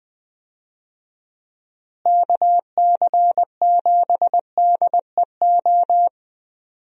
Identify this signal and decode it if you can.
Morse code, heard as a keyed tone, about 20 wpm, 700 Hz. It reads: KC7DEO
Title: USA Call Signs - 6 Characters - Repeated 2x - 20wpm